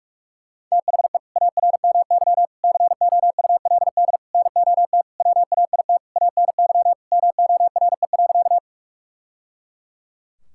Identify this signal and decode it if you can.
Morse code, heard as a keyed tone, about 41 wpm, 690 Hz. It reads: THE ARMY COULD NOT WAIT ANY MORE.
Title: Sentences from Top 1000 Words - Review Part 5 - 45wpm